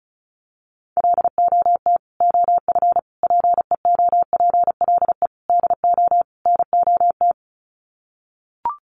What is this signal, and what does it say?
Morse code, heard as a keyed tone, about 34 wpm, 700 Hz. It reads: LOT OF PEOPLE DO NOT
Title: Sets of 5 Words Encore 35wpm